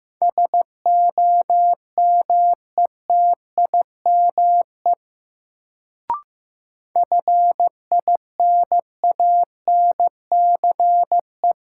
Morse code, heard as a keyed tone, about 16 wpm, 700 Hz. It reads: SOMETIME FINANCE
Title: Sentences from Top 1500 Words - New Word Drill Alt - 15wpm